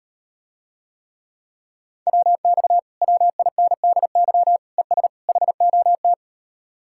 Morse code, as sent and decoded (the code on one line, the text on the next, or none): .-- -..- / .-- .. -. -.. -.-- / . ... / .... --- -
WX WINDY ES HOT